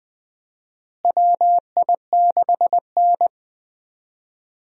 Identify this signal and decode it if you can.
Morse code, heard as a keyed tone, about 20 wpm, 700 Hz. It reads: WI6N
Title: USA Call Signs - 4 Characters - Repeated 3x - 20wpm